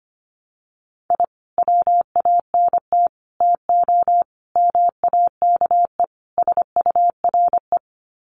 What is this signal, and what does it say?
Morse code, heard as a keyed tone, about 25 wpm, 700 Hz. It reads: I WANT TO MAKE SURE
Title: Sets of 5 Words Encore2 25wpm